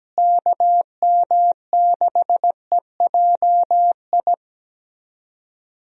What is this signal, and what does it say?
Morse code, heard as a keyed tone, about 17 wpm, 700 Hz. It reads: KM6EJI
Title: USA Call Signs - 6 Characters - Repeated 3x - 17wpm